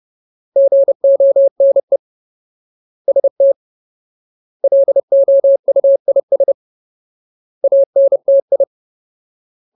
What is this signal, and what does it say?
Morse code, heard as a keyed tone, about 30 wpm, 550 Hz. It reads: GONE ST LOUIS ANTI
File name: Ultimate Word List / 30WPM Ultimate